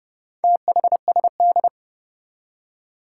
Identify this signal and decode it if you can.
Morse code, heard as a keyed tone, about 31 wpm, 700 Hz. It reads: THSD
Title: Four Challenging Letter-Number Combinations 30wpm